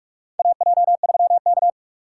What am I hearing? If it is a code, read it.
Morse code, heard as a keyed tone, about 42 wpm, 700 Hz. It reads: AJ3K
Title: USA Call Signs - 6x Character Spacing - 45wpm